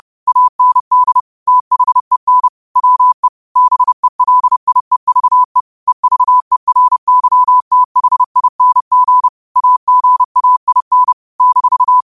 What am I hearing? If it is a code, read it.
Morse code, heard as a keyed tone, about 30 wpm, 1000 Hz. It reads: AND THEN WE BELIEVE EVERYTHING AGAIN -
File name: qotdfiles / 30wpm